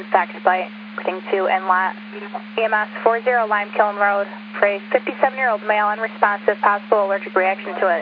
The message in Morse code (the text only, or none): none